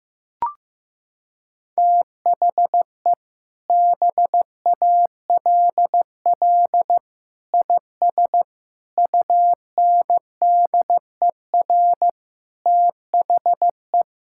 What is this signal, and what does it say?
Morse code, heard as a keyed tone, about 15 wpm, 700 Hz. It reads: THE BALL IS UNDER THE
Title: Sentences from Top 1000 Words - Review Part 6 - 15wpm